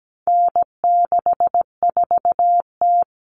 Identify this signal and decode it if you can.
Morse code, heard as a keyed tone, about 17 wpm, 700 Hz. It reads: N64T
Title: Common Cut Numbers - 4-Digit 17wpm